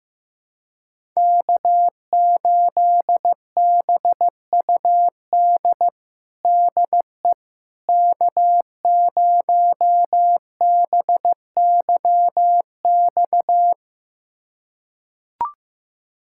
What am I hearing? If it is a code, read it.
Morse code, heard as a keyed tone, about 15 wpm, 700 Hz. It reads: K8BUD DE K0BYX
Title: USA Callsign DE Callsign - 5 Characters - 15wpm